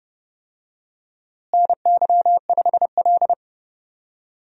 Morse code, heard as a keyed tone, about 30 wpm, 700 Hz. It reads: NY5L
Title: USA Call Signs - 4 Characters - Repeated 2x - 30wpm